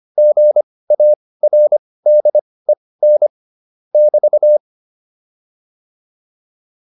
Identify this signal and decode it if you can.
Morse code, heard as a keyed tone, about 25 wpm, spaced as for 18 wpm, 600 Hz. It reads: GARDEN <BT>
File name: QSO410 18